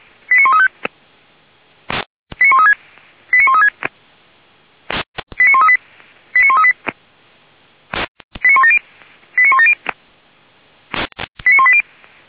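Five-tone selective call (ZVEI), heard as quick runs of five tones, tones of about 80 ms. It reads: 89135 89136 89136 89138 89138 89170 89170 89180
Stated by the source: ZVEI five-tone call received on SDR.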